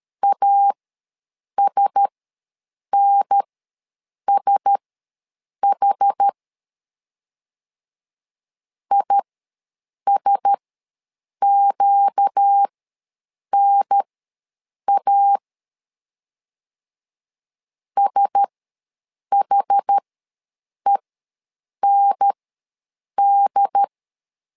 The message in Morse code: ASNSH ISQNA SHEND

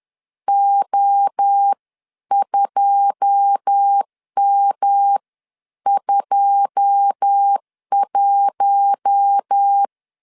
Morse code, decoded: O2M21

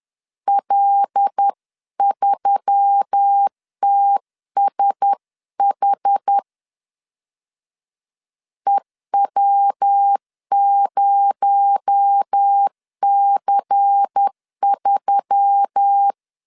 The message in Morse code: L3TSH EW0C3

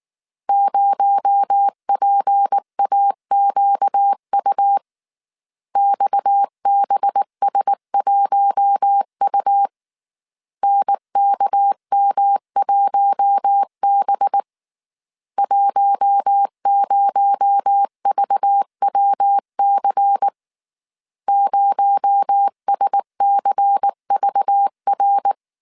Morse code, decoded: 0PAQU XBS1U NKM1B 10VWC 0SCVR

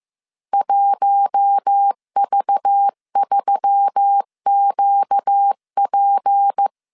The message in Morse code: 1V3QP

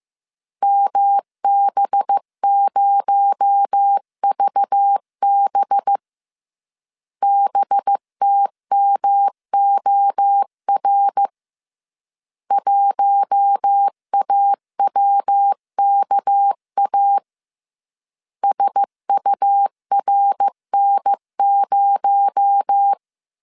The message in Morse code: MB0VB BTMOR 1AWKA SURN0

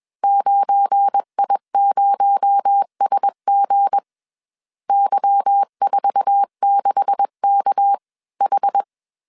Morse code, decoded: 9I0SG Y46KH

